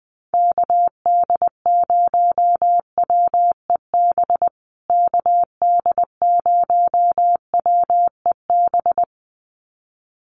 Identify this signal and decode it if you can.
Morse code, heard as a keyed tone, about 20 wpm, 700 Hz. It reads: KD0WEB KD0WEB
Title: USA Call Signs Repeated 2x 20wpm